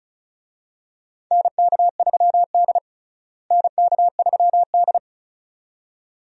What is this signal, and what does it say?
Morse code, heard as a keyed tone, about 36 wpm, 700 Hz. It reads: NK3D NK3D
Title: USA Call Signs - 4 Characters - Repeated 2x - 35wpm